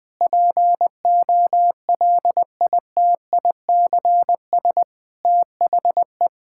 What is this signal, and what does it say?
Morse code, heard as a keyed tone, about 19 wpm, 700 Hz. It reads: POLITICS THE
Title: Sentences from Top 1000 Words - Review Part 6 - 20wpm